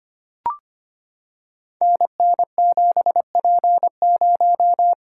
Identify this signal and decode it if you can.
Morse code, heard as a keyed tone, about 25 wpm, 700 Hz. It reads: NN7P0
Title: Code Groups - 5 Letter:Numbers - 25wpm